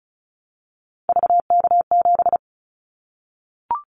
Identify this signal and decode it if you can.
Morse code, heard as a keyed tone, about 36 wpm, 700 Hz. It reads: VK7
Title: Callsign Prefixes for Top 100 DXCC - 35wpm